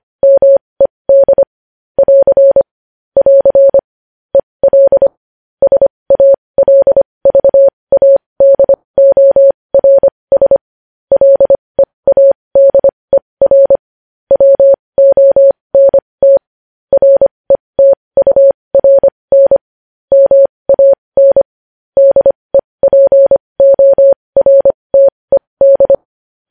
Morse code, heard as a keyed tone, about 25 wpm, spaced as for 20 wpm, 550 Hz. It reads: MED <AR> <AR> EL SALVADORS LEADER WONT RETURN MAN DEPORTED
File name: podcast / News headlines-1x-25x20